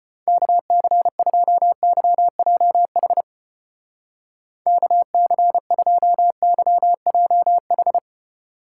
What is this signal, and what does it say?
Morse code, heard as a keyed tone, about 33 wpm, 700 Hz. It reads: KC2YJH KC2YJH